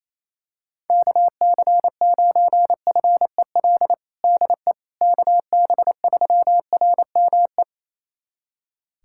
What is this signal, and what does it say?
Morse code, heard as a keyed tone, about 28 wpm, 700 Hz. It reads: KC9FEL DE KB3RME
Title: USA Callsign DE Callsign - 6 Characters - 28wpm